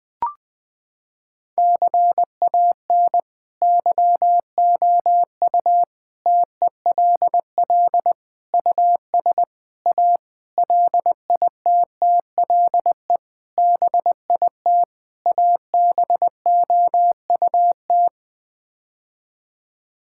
Morse code, heard as a keyed tone, about 21 wpm, 700 Hz. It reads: CAN YOU TELL US A LITTLE BIT ABOUT
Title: Sets of 8 Words - 20wpm